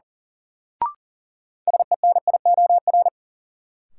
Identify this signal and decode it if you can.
Morse code, heard as a keyed tone, about 37 wpm, 700 Hz. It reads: SENIOR
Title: Sets of 5 Words Encore - New Word Drill - 40wpm